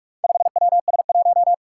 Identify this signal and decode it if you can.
Morse code, heard as a keyed tone, about 44 wpm, 700 Hz. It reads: 5WS1
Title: Four Challenging Letter-Number Combinations 45wpm